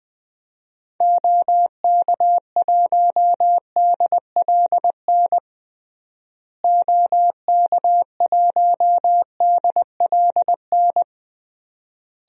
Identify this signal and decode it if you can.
Morse code, heard as a keyed tone, about 20 wpm, 700 Hz. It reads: OK1DLN OK1DLN
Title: DXCC-100 Call Signs - 6 Characters - Repeated 3x - 20wpm